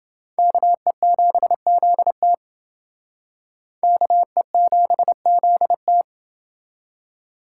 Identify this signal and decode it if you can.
Morse code, heard as a keyed tone, about 28 wpm, 700 Hz. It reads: KE7ZT KE7ZT